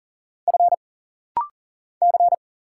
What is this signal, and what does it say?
Morse code, heard as a keyed tone, about 40 wpm, 700 Hz. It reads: F C
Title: Single Letter-Number - Rapid-Fire 40wpm